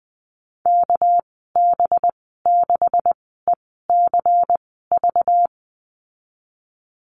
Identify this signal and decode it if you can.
Morse code, heard as a keyed tone, about 19 wpm, spaced as for 14 wpm, 700 Hz. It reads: KB6ECV
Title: 231 Lesson 37g - Letter x - Call Signs 20wpm